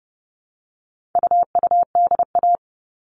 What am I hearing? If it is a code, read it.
Morse code, heard as a keyed tone, about 30 wpm, 700 Hz. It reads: UUDA
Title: All Cut Numbers - 4-Digit 30wpm